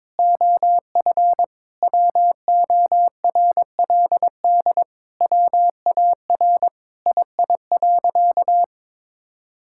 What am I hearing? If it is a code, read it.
Morse code, heard as a keyed tone, about 22 wpm, 700 Hz. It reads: OF WORLD WAR II.
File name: Sentences with 7-Word Phrases - 22wpm